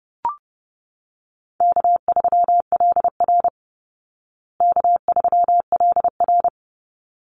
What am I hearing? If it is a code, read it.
Morse code, heard as a keyed tone, about 30 wpm, 700 Hz. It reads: K3LR K3LR
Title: Call Signs Repeated 3x - 30wpm